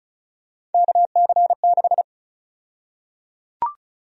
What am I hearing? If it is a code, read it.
Morse code, heard as a keyed tone, about 34 wpm, 700 Hz. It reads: KC6
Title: USA Call Sign Prefixes - 35wpm